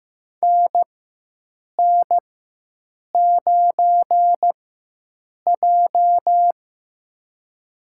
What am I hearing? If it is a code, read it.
Morse code, heard as a keyed tone, about 15 wpm, spaced as for 7 wpm, 700 Hz. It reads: NN9J
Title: USA Call Signs - 4 Characters - 4x Character Spacing - 15wpm